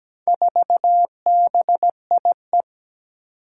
Morse code, heard as a keyed tone, about 17 wpm, 700 Hz. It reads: 4BIE
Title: Four Challenging Letter-Number Combinations 17wpm